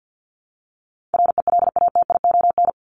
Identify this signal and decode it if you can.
Morse code, heard as a keyed tone, about 50 wpm, 700 Hz. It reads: RELATION
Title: Top 1751-2000 Words 50wpm